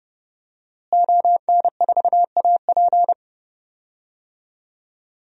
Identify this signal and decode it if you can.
Morse code, heard as a keyed tone, about 30 wpm, 700 Hz. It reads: ON4AP
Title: DXCC-100 Call Signs - 5 Characters - Repeated 2x - 30wpm